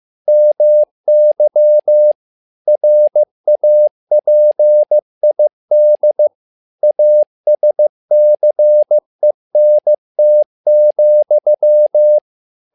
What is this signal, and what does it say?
Morse code, heard as a keyed tone, about 15 wpm, 600 Hz. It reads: MY RAPID ASCENT,
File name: learning center / tower trouble / tower trouble 15wpm